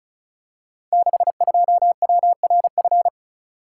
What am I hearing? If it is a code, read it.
Morse code, heard as a keyed tone, about 36 wpm, 690 Hz. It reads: 62WRF
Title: Code Groups - 5 Letter:Numbers - 35wpm